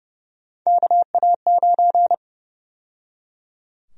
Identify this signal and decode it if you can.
Morse code, heard as a keyed tone, about 30 wpm, 700 Hz. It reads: KA9
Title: USA Call Sign Prefixes - 30wpm